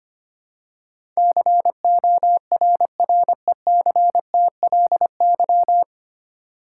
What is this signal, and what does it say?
Morse code, heard as a keyed tone, about 25 wpm, 700 Hz. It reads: CORRECTLY
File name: Nine Letter Words - 25wpm